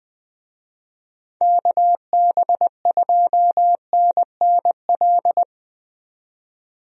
Morse code, heard as a keyed tone, about 20 wpm, 700 Hz. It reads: KB2NNL
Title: USA Call Signs - 6 Characters - 20wpm